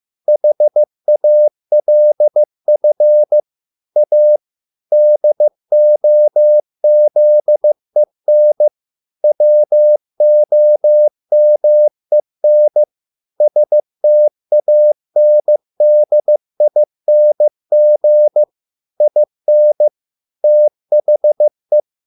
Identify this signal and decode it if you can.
Morse code, heard as a keyed tone, about 15 wpm, 600 Hz. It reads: HALF A DOZEN WOMEN STANDING IN THE